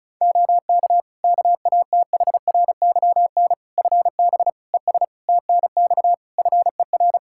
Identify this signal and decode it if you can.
Morse code, heard as a keyed tone, about 34 wpm, 700 Hz. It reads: OK KATHRYN FB ES TNX FER
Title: Practice QSO - Exchange 2 - Info - 35wpm